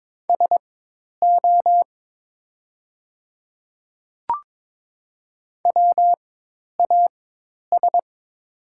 Morse code, heard as a keyed tone, about 22 wpm, spaced as for 10 wpm, 700 Hz. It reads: SO WAS
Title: 4x Character Spacing - Top 100 Words - 22wpm